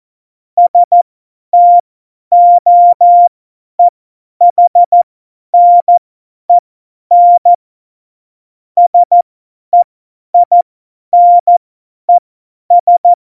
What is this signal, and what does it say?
Morse code, heard as a keyed tone, about 14 wpm, spaced as for 10 wpm, 700 Hz. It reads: STOEHNEN SEINES
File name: Morsekurs / Witze / Witz 15. September 2020, 50 F70